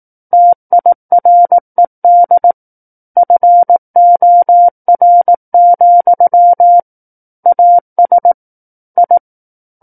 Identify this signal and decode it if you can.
Morse code, heard as a keyed tone, about 18 wpm, 700 Hz. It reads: TIRED FOR, AS I
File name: 18 WPM